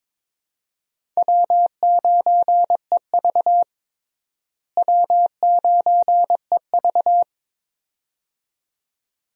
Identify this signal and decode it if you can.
Morse code, heard as a keyed tone, about 21 wpm, 700 Hz. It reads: W9EV W9EV
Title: USA Call Signs - 4 Characters - Repeated 2x - 22wpm